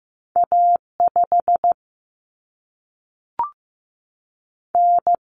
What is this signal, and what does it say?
Morse code, heard as a keyed tone, about 15 wpm, 700 Hz. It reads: A5 N
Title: Common Cut Numbers - 2-Digit 15wpm